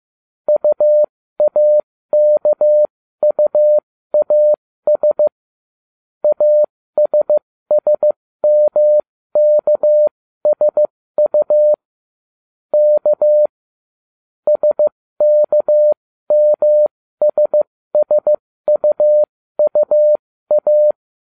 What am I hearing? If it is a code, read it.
Morse code, heard as a keyed tone, about 15 wpm, spaced as for 13 wpm, 600 Hz. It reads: UAKUAS ASSMKSU K SKMSSUUA